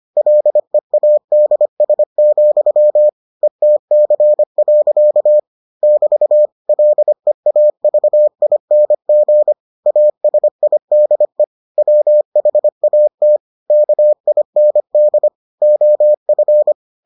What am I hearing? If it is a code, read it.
Morse code, heard as a keyed tone, about 25 wpm, 600 Hz. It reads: LEADS, ETC. <BT> LEAVING ASIDE WHAT KIND OF